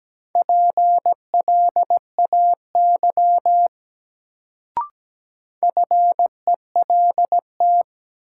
Morse code, heard as a keyed tone, about 17 wpm, 700 Hz. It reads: PLAY FELT
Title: Sets of 5 Words Encore - New Word Drill - 17wpm